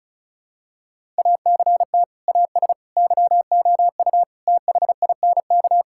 Morse code, heard as a keyed tone, about 36 wpm, 700 Hz. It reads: ACT AS YOU THINK
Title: Sentences from Top 1000 Words - Review Part 5 - 35wpm